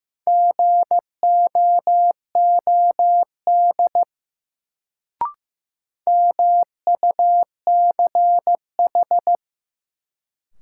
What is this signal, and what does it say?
Morse code, heard as a keyed tone, about 15 wpm, 700 Hz. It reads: GOOD MUCH
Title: Sentences from Top 200 Words - New Word Drill - 15wpm